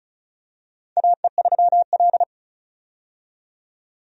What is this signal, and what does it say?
Morse code, heard as a keyed tone, about 36 wpm, 700 Hz. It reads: AE3L
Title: USA Call Signs - 4 Characters - 35wpm